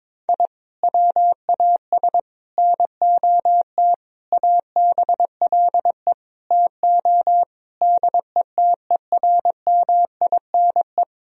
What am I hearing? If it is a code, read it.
Morse code, heard as a keyed tone, about 22 wpm, 700 Hz. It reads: I WAS NOT ABLE TO DETERMINE
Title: Sentences from Top 1000 Words - Review Part 4 - 22wpm